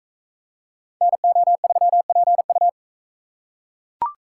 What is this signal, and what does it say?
Morse code, heard as a keyed tone, about 40 wpm, 700 Hz. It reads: NO3PU